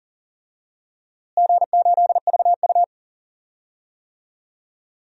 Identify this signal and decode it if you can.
Morse code, heard as a keyed tone, about 40 wpm, 700 Hz. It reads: G8VU
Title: DXCC-100 Call Signs - 4 Characters - Repeated 2x - 40wpm